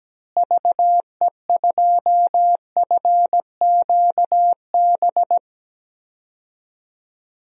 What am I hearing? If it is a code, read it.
Morse code, heard as a keyed tone, about 17 wpm, 700 Hz. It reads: VE2FQB
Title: DXCC-100 Call Signs - 6 Characters - Repeated 2x - 17wpm